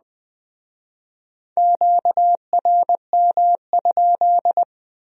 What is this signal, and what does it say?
Morse code, heard as a keyed tone, about 21 wpm, 700 Hz. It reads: QRM?
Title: Speed-Racing Q-Codes - 17wpm